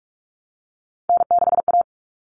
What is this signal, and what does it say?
Morse code, heard as a keyed tone, about 44 wpm, 700 Hz. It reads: N6A
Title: All Cut Numbers - 3-Digit 45wpm